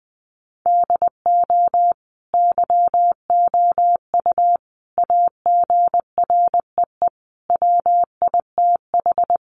Morse code, heard as a keyed tone, about 20 wpm, 700 Hz. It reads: DO YOU AGREE WITH
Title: Sets of 5 Words Encore 20wpm